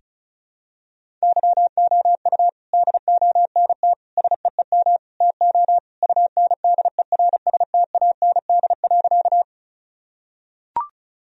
Morse code, heard as a keyed tone, about 36 wpm, 700 Hz. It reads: YOU DONT SEEM TO UNDERSTAND.
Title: Sentences from Top 1000 Words - Review Part 1 - 35wpm